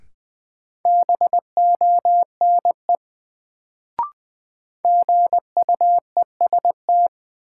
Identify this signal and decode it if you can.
Morse code, heard as a keyed tone, about 20 wpm, 700 Hz. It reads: BONE GUEST
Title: Sentences from Top 1600 Words - New Word Drill - 20wpm